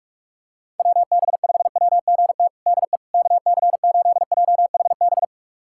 Morse code, heard as a keyed tone, about 46 wpm, 700 Hz. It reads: WB5WGT DE KC8JHB
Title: USA Callsign DE Callsign 45wpm